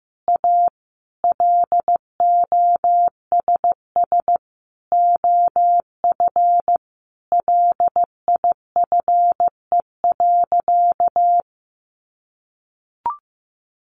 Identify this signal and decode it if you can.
Morse code, heard as a keyed tone, about 15 wpm, 700 Hz. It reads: A LOSS OF LIFE.
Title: Sentences from Top 1400 Words 15wpm